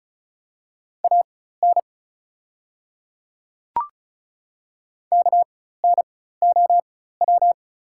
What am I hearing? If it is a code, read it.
Morse code, heard as a keyed tone, about 36 wpm, spaced as for 16 wpm, 700 Hz. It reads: AN KNOW
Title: 4x Character Spacing - Top 100 Words - 35wpm